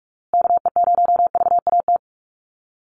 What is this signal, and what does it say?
Morse code, heard as a keyed tone, about 44 wpm, 700 Hz. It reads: KE0VAT
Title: USA Call Signs Repeated 2x 45wpm